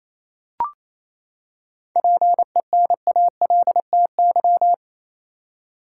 Morse code, heard as a keyed tone, about 28 wpm, 700 Hz. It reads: PENALTY
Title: Seven Letter Words - 28wpm